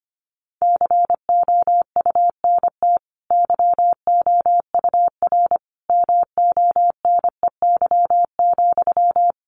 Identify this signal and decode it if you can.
Morse code, heard as a keyed tone, about 25 wpm, 700 Hz. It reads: COUNT YOUR MONEY,
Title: Sentences from Top 1500 Words 25wpm